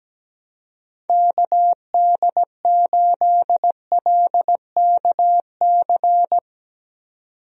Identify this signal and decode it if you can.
Morse code, heard as a keyed tone, about 17 wpm, 700 Hz. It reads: KD8LKC
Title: USA Call Signs - 6 Characters - 17wpm